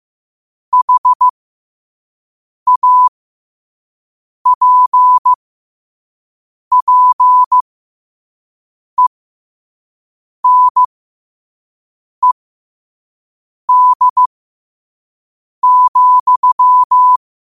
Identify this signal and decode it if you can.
Morse code, heard as a keyed tone, about 15 wpm, spaced as for 5 wpm, 1000 Hz. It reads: HAPPENED,